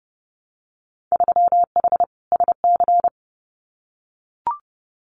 Morse code, heard as a keyed tone, about 30 wpm, 700 Hz. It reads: 3H SC
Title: Winter Field Day Exchanges 30wpm